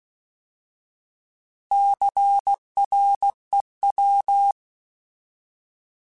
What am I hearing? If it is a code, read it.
Morse code, heard as a keyed tone, about 16 wpm, 800 Hz. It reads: CREW
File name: Alphabetical / Alphabetical mp3s / Cs @15